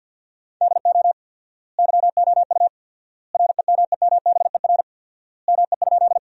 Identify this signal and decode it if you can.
Morse code, heard as a keyed tone, about 50 wpm, 700 Hz. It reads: DO YOU REMEMBER ME?
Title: 4x Word-Spacing - Sentences from Top 400 Words - 50wpm